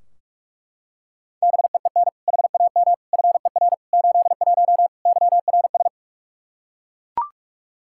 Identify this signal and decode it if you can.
Morse code, heard as a keyed tone, about 41 wpm, 700 Hz. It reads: BEEN HAM FER 81 YRS
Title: Practice QSO - Exchange 3 - More Info - 45wpm